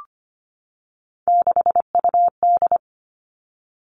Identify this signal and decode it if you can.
Morse code, heard as a keyed tone, about 25 wpm, 700 Hz. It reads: 6UD
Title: Three Challenging Letter-Number Combinations 25wpm